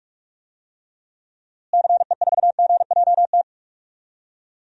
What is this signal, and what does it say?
Morse code, heard as a keyed tone, about 46 wpm, 700 Hz. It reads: CE4GJT
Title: DXCC-100 Call Signs - 6 Characters - Repeated 3x - 45wpm